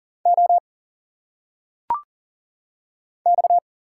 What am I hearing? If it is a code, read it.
Morse code, heard as a keyed tone, about 38 wpm, 700 Hz. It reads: O X
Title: Single Letters - Phonetic Pronunciation 40wpm